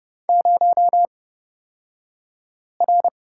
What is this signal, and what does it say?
Morse code, heard as a keyed tone, about 30 wpm, spaced as for 5 wpm, 700 Hz. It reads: 0R
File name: Lesson 37d - Letter x - Two Characters 30wpm